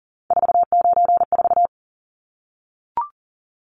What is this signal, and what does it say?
Morse code, heard as a keyed tone, about 40 wpm, 700 Hz. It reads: 494
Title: Numbers - 3-Digit 40wpm